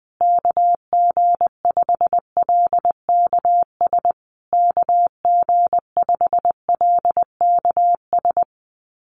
Morse code, heard as a keyed tone, about 20 wpm, 700 Hz. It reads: KG5LKS KG5LKS
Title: USA Call Signs Repeated 3x 20wpm